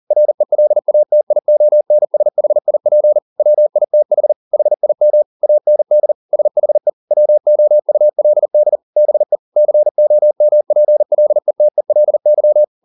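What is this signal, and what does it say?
Morse code, heard as a keyed tone, about 40 wpm, 600 Hz. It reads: RELATIONSHIP WITH HIM AND SHE WOULD BE COMPLETELY
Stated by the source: Morse code at 40 wpm, practice text, CW med Gnister.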